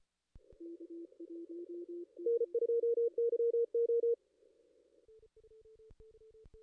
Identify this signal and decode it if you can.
Morse code, heard as a keyed tone, about 28 wpm, 470 Hz. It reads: N2YO
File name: contests / ss-cw-k5tr-op-n5rz -2300